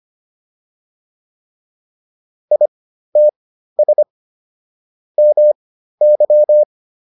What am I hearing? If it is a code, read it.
Morse code, heard as a keyed tone, about 25 wpm, spaced as for 13 wpm, 600 Hz. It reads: ITS MY